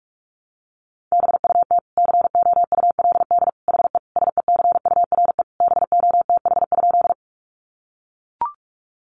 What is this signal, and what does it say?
Morse code, heard as a keyed tone, about 44 wpm, 700 Hz. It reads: BUT COULD HE SECURE BOTH?
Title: Sentences from Top 1200 Words 45wpm